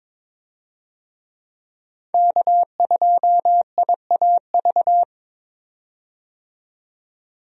Morse code, heard as a keyed tone, about 22 wpm, 700 Hz. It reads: K2IAV
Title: USA Call Signs - 5 Characters - 22wpm